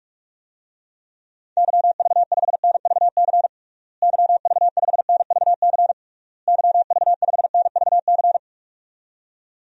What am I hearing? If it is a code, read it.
Morse code, heard as a keyed tone, about 46 wpm, 700 Hz. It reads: YV5NVC YV5NVC YV5NVC
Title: DXCC-100 Call Signs - 6 Characters - Repeated 3x - 45wpm